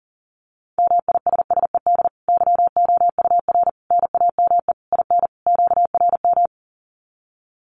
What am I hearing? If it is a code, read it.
Morse code, heard as a keyed tone, about 40 wpm, 700 Hz. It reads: MISSED YOUR NAME IN QRM
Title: Common QSO Phrases 40wpm